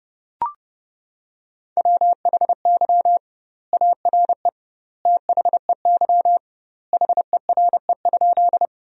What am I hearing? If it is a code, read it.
Morse code, heard as a keyed tone, about 30 wpm, 700 Hz. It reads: WHY ARE THEY HERE?
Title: 2x Word-Spacing - Sentences from Top 200 Words - 30wpm